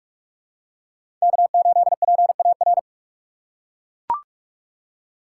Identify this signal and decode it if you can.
Morse code, heard as a keyed tone, about 41 wpm, 700 Hz. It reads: K8PAR
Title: USA Call Signs - 5 Characters - 6x Character Spacing - 45wpm